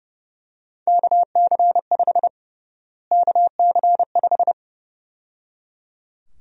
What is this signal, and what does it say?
Morse code, heard as a keyed tone, about 29 wpm, 700 Hz. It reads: KC5 KC5
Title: USA Call Sign Prefixes - Repeated 2x - 30wpm